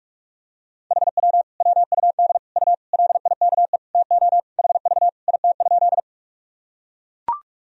Morse code, heard as a keyed tone, about 43 wpm, 700 Hz. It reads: HW WUD U LIKE TO HV IT?
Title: Sentences from Top 100 Words and Abbreviations 45wpm